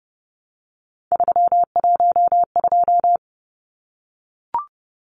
Morse code, heard as a keyed tone, about 30 wpm, 700 Hz. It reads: 312
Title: Numbers - 3-Digit 30wpm